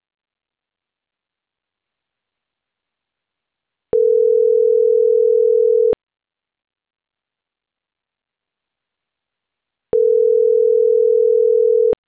Ringback tone (North America), 440 + 480 Hz.